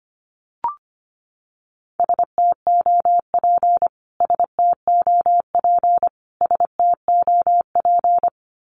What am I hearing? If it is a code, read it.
Morse code, heard as a keyed tone, about 25 wpm, 700 Hz. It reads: STOP STOP STOP
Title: Top 300 Words Repeated 25wpm